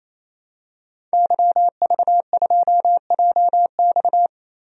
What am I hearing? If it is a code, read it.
Morse code, heard as a keyed tone, about 28 wpm, 700 Hz. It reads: YV2JX
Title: DXCC-100 Call Signs - 5 Characters - Repeated 3x - 28wpm